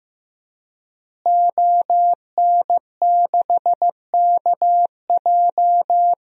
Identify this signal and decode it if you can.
Morse code, heard as a keyed tone, about 15 wpm, 700 Hz. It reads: ON6KJ
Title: DXCC-100 Call Signs - 5 Characters - Repeated 2x - 15wpm